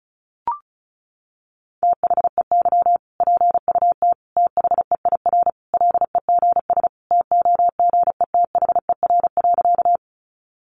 Morse code, heard as a keyed tone, about 34 wpm, 700 Hz. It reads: THEY PUT THEIR LEGS TOGETHER.
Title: Easy Phrases 35wpm